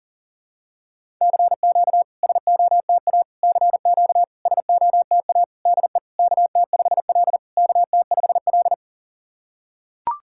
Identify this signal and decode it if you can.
Morse code, heard as a keyed tone, about 40 wpm, 700 Hz. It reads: CQ SOTA CQ SOTA DE KT5L KT5L
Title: SOTA QSO Practice 40wpm